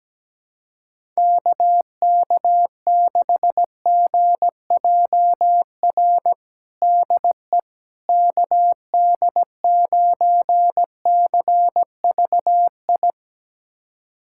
Morse code, heard as a keyed tone, about 17 wpm, 700 Hz. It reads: KK6GJR DE KD9CVI
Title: USA Callsign DE Callsign - 6 Characters - 17wpm